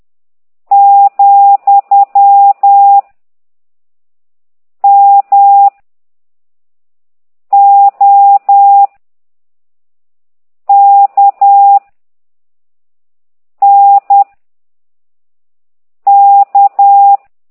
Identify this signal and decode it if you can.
Morse code, heard as a keyed tone, about 10 wpm, 800 Hz. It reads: , M O K N K
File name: ct2idw / exerc15